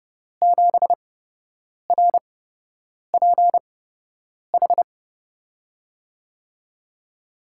Morse code, heard as a keyed tone, about 30 wpm, spaced as for 8 wpm, 700 Hz. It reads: 7RPH
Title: USA Call Signs - 6 Characters - 8x Character Spacing - 30wpm